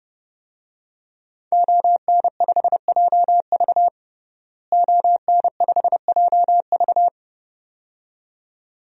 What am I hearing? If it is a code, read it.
Morse code, heard as a keyed tone, about 30 wpm, 700 Hz. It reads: ON5JV ON5JV
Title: DXCC-100 Call Signs - 5 Characters - Repeated 2x - 30wpm